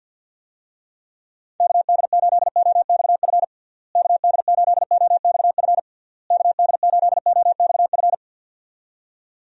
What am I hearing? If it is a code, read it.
Morse code, heard as a keyed tone, about 50 wpm, 700 Hz. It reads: KD8OXF KD8OXF KD8OXF
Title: USA Call Signs - 6 Characters - Repeated 3x - 50wpm